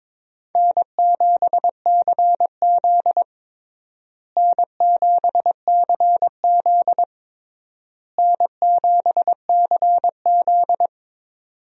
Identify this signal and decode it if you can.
Morse code, heard as a keyed tone, about 21 wpm, 700 Hz. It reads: N7CZ N7CZ N7CZ
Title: USA Call Signs - 4 Characters - Repeated 3x - 22wpm